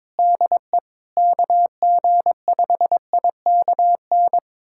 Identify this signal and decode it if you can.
Morse code, heard as a keyed tone, about 22 wpm, 700 Hz. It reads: DE KG5IKN
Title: USA Callsign DE Callsign - 6 Characters - 22wpm